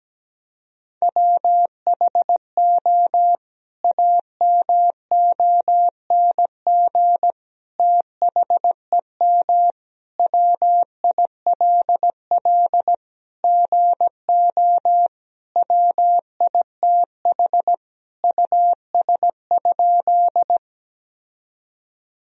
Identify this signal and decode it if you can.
Morse code, heard as a keyed tone, about 17 wpm, 700 Hz. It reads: WHO AMONG THEM WILL GO WITH US?
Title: Sentences from Top 1000 Words - Review Part 3 - 17wpm